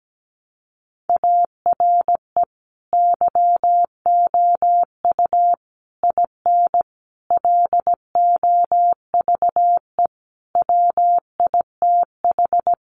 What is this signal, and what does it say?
Morse code, heard as a keyed tone, about 17 wpm, 700 Hz. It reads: ARE YOU IN LOVE WITH
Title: Sentences from Top 400 Words 17wpm